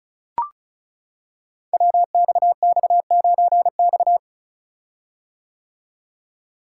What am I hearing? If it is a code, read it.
Morse code, heard as a keyed tone, about 36 wpm, 700 Hz. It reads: WXX9X
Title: Code Groups - 5 Letter:Numbers - 35wpm